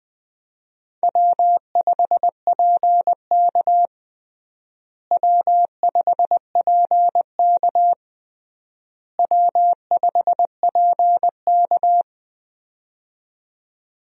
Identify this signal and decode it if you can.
Morse code, heard as a keyed tone, about 21 wpm, 700 Hz. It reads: W5PK W5PK W5PK
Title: USA Call Signs - 4 Characters - Repeated 3x - 20wpm